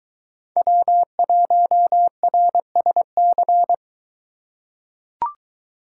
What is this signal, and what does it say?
Morse code, heard as a keyed tone, about 22 wpm, 700 Hz. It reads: W1RSC